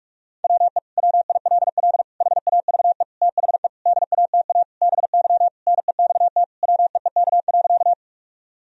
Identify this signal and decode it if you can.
Morse code, heard as a keyed tone, about 46 wpm, 700 Hz. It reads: WE WILL HAVE THE DATA BY NEXT WEEK.
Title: Sentences from Top 1000 Words - Review Part 3 - 45wpm